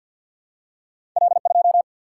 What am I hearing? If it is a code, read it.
Morse code, heard as a keyed tone, about 49 wpm, 700 Hz. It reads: L 2
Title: Single-Letter Single-Number 50wpm